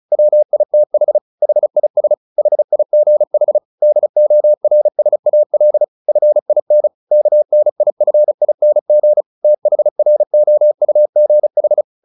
Morse code, heard as a keyed tone, about 34 wpm, 600 Hz. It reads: WITH HIS HIGH DORSAL FIN KNIFING THROUGH
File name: the old man and the sea 2 35wpm